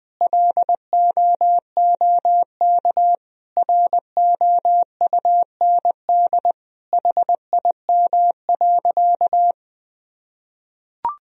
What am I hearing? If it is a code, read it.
Morse code, heard as a keyed tone, about 20 wpm, 700 Hz. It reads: LOOK ROUND HIM.
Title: Sentences from Top 1000 Words - Review Part 1 - 20wpm